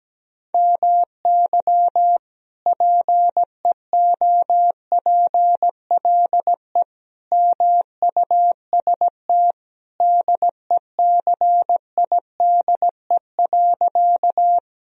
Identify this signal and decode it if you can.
Morse code, heard as a keyed tone, about 17 wpm, 700 Hz. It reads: MY PEOPLE MUST DECIDE.
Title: Sentences from Top 1000 Words - Review Part 3 - 17wpm